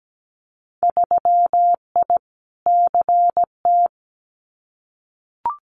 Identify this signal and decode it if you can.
Morse code, heard as a keyed tone, about 17 wpm, 690 Hz. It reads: 3I CT
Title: Winter Field Day Exchanges 17wpm